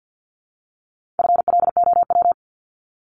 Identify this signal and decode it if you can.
Morse code, heard as a keyed tone, about 53 wpm, 700 Hz. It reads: FLOW
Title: Top 1001-1250 Words 50wpm